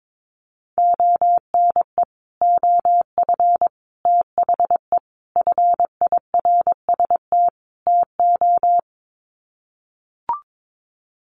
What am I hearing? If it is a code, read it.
Morse code, heard as a keyed tone, about 22 wpm, 700 Hz. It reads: ONE OF THE FIRST TO
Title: Sets of 5 Words Encore 22wpm